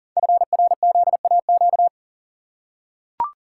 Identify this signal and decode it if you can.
Morse code, heard as a keyed tone, about 42 wpm, 700 Hz. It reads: FRZAQ
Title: Code Groups - 5 Letters - 40wpm